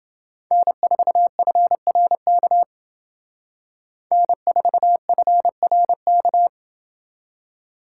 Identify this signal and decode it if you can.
Morse code, heard as a keyed tone, about 28 wpm, 700 Hz. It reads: N4FRK N4FRK